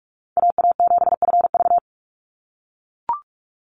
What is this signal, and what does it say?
Morse code, heard as a keyed tone, about 44 wpm, 700 Hz. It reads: AA7FV
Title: Call Signs - 45wpm